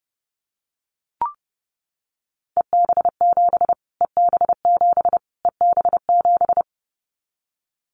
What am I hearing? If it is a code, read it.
Morse code, heard as a keyed tone, about 28 wpm, 700 Hz. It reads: EB7 EB7 EB7
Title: Callsign Prefixes for Top 25 DXCC - Repeated 3x - 30wpm